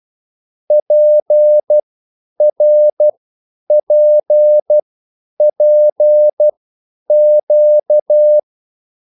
Morse code, heard as a keyed tone, about 12 wpm, spaced as for 9 wpm, 600 Hz. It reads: PRPPQ
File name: QSL / CW lesson